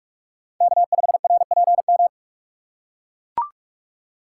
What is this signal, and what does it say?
Morse code, heard as a keyed tone, about 43 wpm, 700 Hz. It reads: K5RPM
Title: USA Call Signs - 5 Characters - 2x Character Spacing - 45wpm